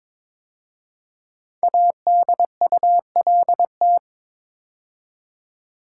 Morse code, heard as a keyed tone, about 22 wpm, 700 Hz. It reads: ADULT
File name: 5 Letter Words - First 3 Letters Given - 22wpm